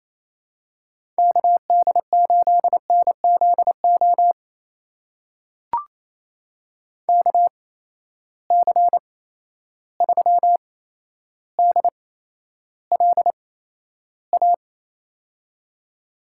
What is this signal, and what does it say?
Morse code, heard as a keyed tone, about 28 wpm, spaced as for 12 wpm, 700 Hz. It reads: KD8NZO KC3DLA
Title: USA Call Signs - 6 Characters - 8x Character Spacing - 28wpm